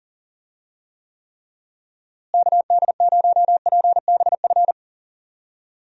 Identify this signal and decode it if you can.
Morse code, heard as a keyed tone, about 40 wpm, 700 Hz. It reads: KD0PBF
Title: USA Call Signs - 6 Characters - Repeated 3x - 40wpm